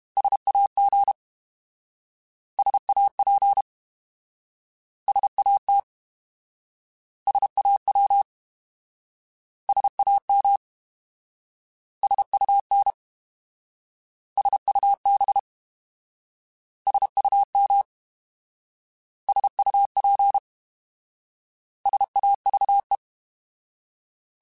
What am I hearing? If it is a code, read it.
Morse code, heard as a keyed tone, about 32 wpm, 800 Hz. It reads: SAG SAP SAT SAW SAM SUN SUB SUM SUP SAVE